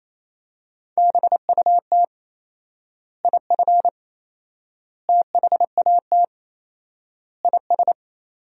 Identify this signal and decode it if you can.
Morse code, heard as a keyed tone, about 28 wpm, 700 Hz. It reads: BUT IF THAT IS
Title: 4x Word-Spacing - Sentences from Top 200 Words - 28wpm